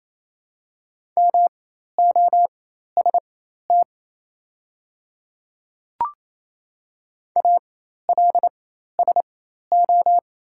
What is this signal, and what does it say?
Morse code, heard as a keyed tone, about 28 wpm, spaced as for 13 wpm, 700 Hz. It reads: MOST ALSO
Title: 4x Character Spacing - Top 200 Words - 28wpm